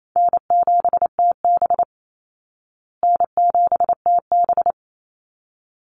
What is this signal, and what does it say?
Morse code, heard as a keyed tone, about 28 wpm, 700 Hz. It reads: N7TB N7TB
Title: Easy Call Signs Repeated 2x - 28wpm